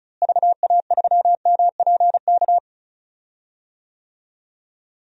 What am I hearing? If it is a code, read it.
Morse code, heard as a keyed tone, about 36 wpm, 700 Hz. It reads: VA3MPK
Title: DXCC-100 Call Signs - 6 Characters - Repeated 2x - 35wpm